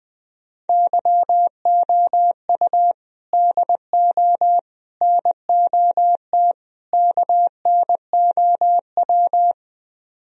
Morse code, heard as a keyed tone, about 20 wpm, 700 Hz. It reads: YOU DO NOT KNOW
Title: Speed-Racing - Sets of 4 Words 20wpm